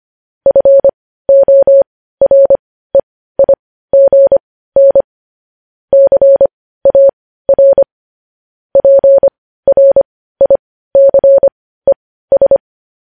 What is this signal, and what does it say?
Morse code, heard as a keyed tone, about 25 wpm, spaced as for 15 wpm, 550 Hz. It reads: FOREIGN CAR PRICES
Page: news-headlines-in-morse-code-at-15-wpm